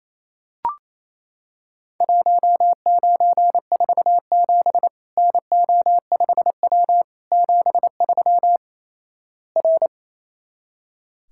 Morse code, heard as a keyed tone, about 28 wpm, 700 Hz. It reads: 1947 NO5W 73 R
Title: Worked All Europe DX Contest - QTC Exchange - 28wpm